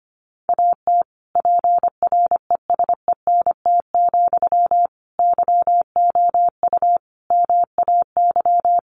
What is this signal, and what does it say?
Morse code, heard as a keyed tone, about 25 wpm, 700 Hz. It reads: AT PRESENT, YOU MAY
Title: Sentences from Top 800 Words 25wpm